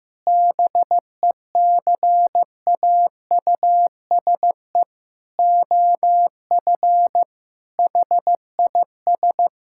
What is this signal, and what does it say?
Morse code, heard as a keyed tone, about 15 wpm, 700 Hz. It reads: BECAUSE OF HIS
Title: Sentences from Top 1600 Words 15wpm